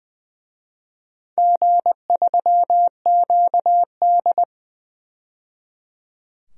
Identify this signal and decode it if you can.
Morse code, heard as a keyed tone, about 20 wpm, 700 Hz. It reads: G3QD
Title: DXCC-100 Call Signs - 4 Characters - 20wpm